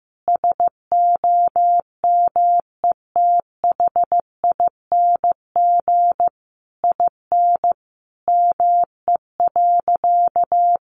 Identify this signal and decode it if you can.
Morse code, heard as a keyed tone, about 15 wpm, 700 Hz. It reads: SOMETHING IN ME.
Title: Sentences from Top 600 Words 15wpm